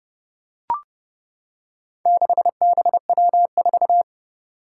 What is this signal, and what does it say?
Morse code, heard as a keyed tone, about 30 wpm, 700 Hz. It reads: 6BW4
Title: Four Challenging Letter-Number Combinations 30wpm